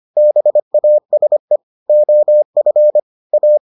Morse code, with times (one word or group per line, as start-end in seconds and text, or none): 0.2-1.6 BASE
1.9-3.0 OF
3.3-3.6 A